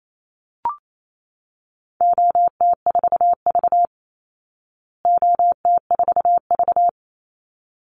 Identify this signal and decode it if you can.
Morse code, heard as a keyed tone, about 28 wpm, 700 Hz. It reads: OT4V OT4V
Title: Call Signs Repeated 3x - 28wpm